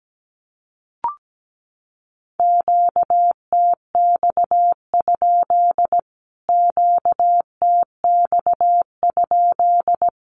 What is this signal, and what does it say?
Morse code, heard as a keyed tone, about 17 wpm, 700 Hz. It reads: QTX? QTX?
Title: Q-Codes Repeated 3x 17wpm